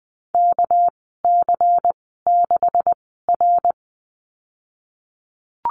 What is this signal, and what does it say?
Morse code, heard as a keyed tone, about 20 wpm, spaced as for 15 wpm, 700 Hz. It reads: KC6R
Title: 207 Lesson 33g - Letter j - Call Signs 20wpm